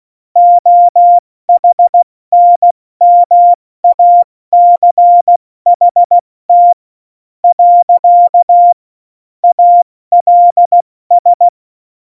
Morse code, heard as a keyed tone, about 16 wpm, 700 Hz. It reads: OHNMACHT . ALS
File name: Morsekurs / Witze / Witz 09. Oktober 2018, 70 F80